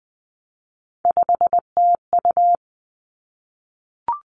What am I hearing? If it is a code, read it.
Morse code, heard as a keyed tone, about 20 wpm, 700 Hz. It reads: 5TU
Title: Three Challenging Letter-Number Combinations 20wpm